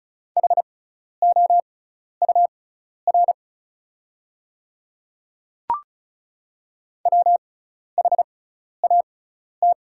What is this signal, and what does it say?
Morse code, heard as a keyed tone, about 34 wpm, spaced as for 12 wpm, 690 Hz. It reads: HOUR WHAT
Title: 6x Character Spacing - Top 500 Words - 35wpm